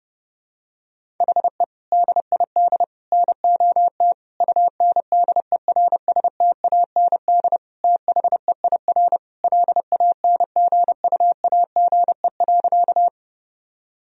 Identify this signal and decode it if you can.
Morse code, heard as a keyed tone, about 32 wpm, 700 Hz. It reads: HE DID NOT UNDERSTAND THEIR LANGUAGE.
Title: Sentences from Top 1000 Words - Review Part 3 - 30wpm